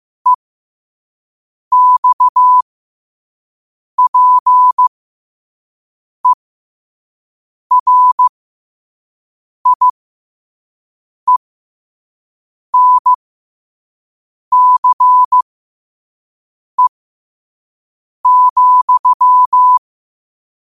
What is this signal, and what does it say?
Morse code, heard as a keyed tone, about 15 wpm, spaced as for 5 wpm, 1000 Hz. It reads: EXPERIENCE,